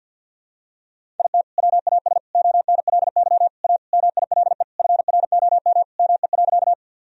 Morse code, heard as a keyed tone, about 52 wpm, 700 Hz. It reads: IT WAS ONLY A MILE FROM ME.
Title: Sentences from Top 1000 Words - Review Part 1 - 50wpm